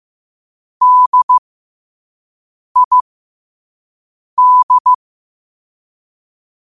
Morse code, heard as a keyed tone, about 15 wpm, spaced as for 5 wpm, 1000 Hz. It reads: DID
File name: qotdfiles / 5wpm